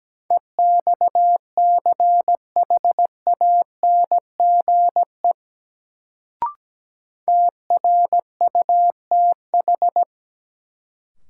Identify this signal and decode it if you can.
Morse code, heard as a keyed tone, about 17 wpm, 700 Hz. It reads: EXCHANGE TRUTH
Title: Sets of 5 Words - New Word Drill - 17wpm